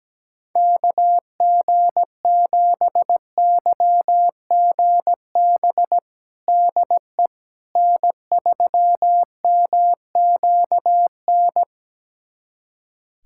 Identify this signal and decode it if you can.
Morse code, heard as a keyed tone, about 17 wpm, 700 Hz. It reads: KG7YGB DE N3MQN
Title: USA Callsign DE Callsign 17wpm